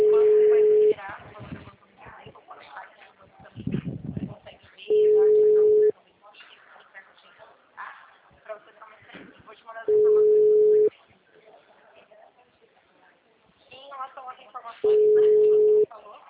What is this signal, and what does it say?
Ringback tone (Europe, 425 Hz), with speech.
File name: out- -2002